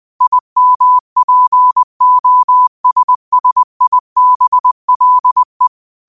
Morse code, heard as a keyed tone, about 20 wpm, 1000 Hz. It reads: IMPOSSIBLE
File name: qotdfiles / 20wpm